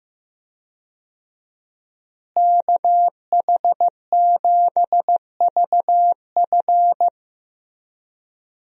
Morse code, heard as a keyed tone, about 16 wpm, 700 Hz. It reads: KH7VF
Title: USA Call Signs - 5 Characters - Repeated 2x - 15wpm